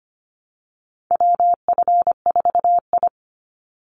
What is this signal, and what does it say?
Morse code, heard as a keyed tone, about 25 wpm, 700 Hz. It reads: WF4I
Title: NA SOTA Chaser Callsigns 25wpm